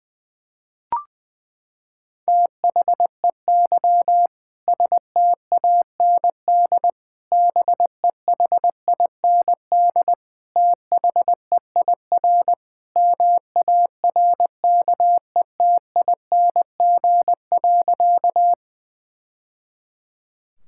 Morse code, heard as a keyed tone, about 21 wpm, 700 Hz. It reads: THEY STAND BEHIND THEIR MARKETING.
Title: Sentences from Top 1000 Words - Review Part 5 - 20wpm